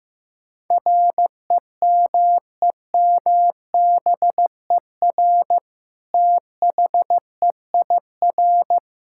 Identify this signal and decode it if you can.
Morse code, heard as a keyed tone, about 15 wpm, 700 Hz. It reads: REMEMBER THEIR
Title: Sentences from Top 1000 Words - Review Part 5 - 15wpm